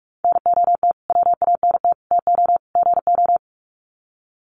Morse code, heard as a keyed tone, about 44 wpm, 700 Hz. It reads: NOT WANT TO GO
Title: Sets of 4 Words 45wpm